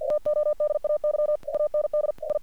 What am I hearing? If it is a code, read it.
Morse code, heard as a keyed tone, about 48 wpm, 600 Hz. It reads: TODAY AND I
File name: archived / CRSnet